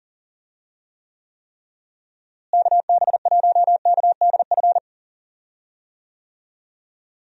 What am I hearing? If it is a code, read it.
Morse code, heard as a keyed tone, about 40 wpm, 700 Hz. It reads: KB1KDF
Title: USA Call Signs - 6 Characters - Repeated 3x - 40wpm